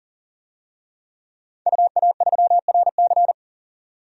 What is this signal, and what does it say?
Morse code, heard as a keyed tone, about 38 wpm, 700 Hz. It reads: UA3RC
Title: DXCC-100 Call Signs - 5 Characters - 40wpm